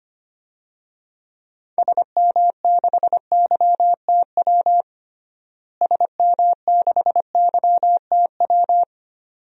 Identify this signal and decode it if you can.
Morse code, heard as a keyed tone, about 25 wpm, 700 Hz. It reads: SM6YTW SM6YTW
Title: DXCC-100 Call Signs - 6 Characters - Repeated 2x - 25wpm